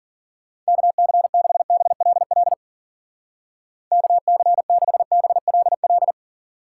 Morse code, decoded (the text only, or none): KC6BLL KC6BLL